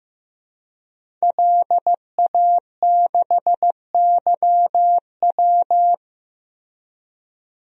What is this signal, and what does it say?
Morse code, heard as a keyed tone, about 15 wpm, 700 Hz. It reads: LA6YW
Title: DXCC-100 Call Signs - 5 Characters - 15wpm